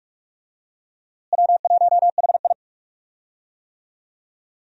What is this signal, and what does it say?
Morse code, heard as a keyed tone, about 46 wpm, 700 Hz. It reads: W1HI
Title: USA Call Signs - 4 Characters - Repeated 3x - 45wpm